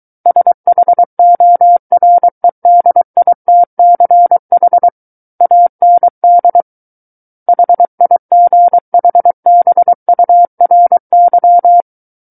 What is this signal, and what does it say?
Morse code, heard as a keyed tone, about 23 wpm, 700 Hz. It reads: SHOREDITCH AND HIGHBURY
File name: Book One / 14-Chapter 14 - 23 WPM